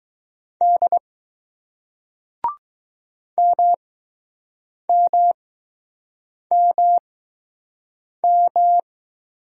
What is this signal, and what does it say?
Morse code, heard as a keyed tone, about 20 wpm, spaced as for 6 wpm, 700 Hz. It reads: D MMMM